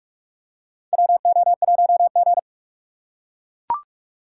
Morse code, heard as a keyed tone, about 44 wpm, 700 Hz. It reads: WO1G
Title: USA Call Signs - 6x Character Spacing - 45wpm